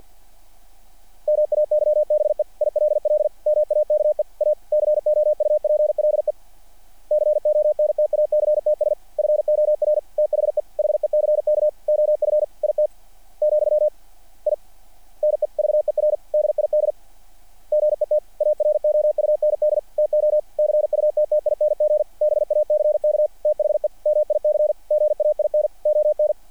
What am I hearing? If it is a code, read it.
Morse code, heard as a keyed tone, about 48 wpm, 600 Hz. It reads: MAYBE ILL MAKE A COUPLE CONTACTS FOR THE HECK OF IT , I NEVER DID GET AROUND TO CUTTING BACK THE MIC GAIN ON
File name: archived / CRSnet